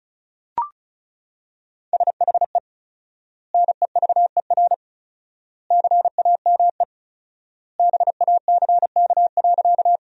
Morse code, heard as a keyed tone, about 34 wpm, 700 Hz. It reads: SHE NEVER CAME BACK.
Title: 4x Word-Spacing - Sentences from Top 300 Words - 35wpm